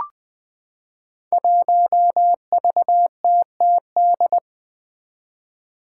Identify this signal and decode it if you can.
Morse code, heard as a keyed tone, about 20 wpm, 700 Hz. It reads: 1VTTD
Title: Five Challenging Letter-Number Combinations 20wpm